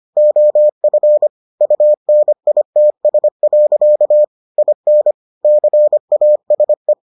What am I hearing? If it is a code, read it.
Morse code, heard as a keyed tone, about 25 wpm, 600 Hz. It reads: OF UNITS. IN CASE